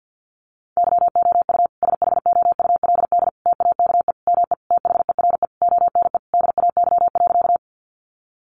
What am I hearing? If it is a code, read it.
Morse code, heard as a keyed tone, about 50 wpm, 700 Hz. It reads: YOU SHOULD TAKE ME THERE ONE DAY.
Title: Sentences from Top 300 Words 50wpm